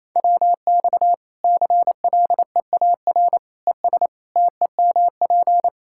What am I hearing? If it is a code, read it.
Morse code, heard as a keyed tone, about 28 wpm, 700 Hz. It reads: WX CLEAR ES TEMP
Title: Practice QSO - Exchange 2 - Info - 28wpm